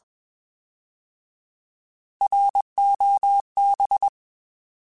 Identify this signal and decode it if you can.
Morse code, heard as a keyed tone, about 21 wpm, 800 Hz. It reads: ROB